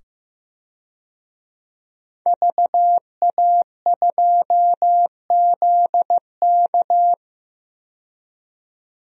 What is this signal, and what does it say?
Morse code, heard as a keyed tone, about 15 wpm, 700 Hz. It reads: VA2ZK
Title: DXCC-100 Call Signs - 5 Characters - 15wpm